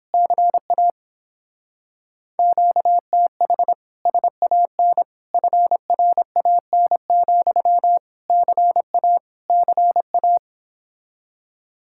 Morse code, heard as a keyed tone, about 29 wpm, 700 Hz. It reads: CA QTH SAN FRAN, CA CA